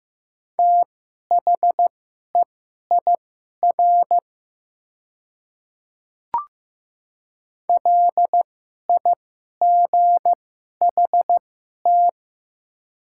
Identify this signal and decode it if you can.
Morse code, heard as a keyed tone, about 15 wpm, spaced as for 11 wpm, 690 Hz. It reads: THEIR LIGHT
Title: 2x Character Spacing - Top 200 Words - 15wpm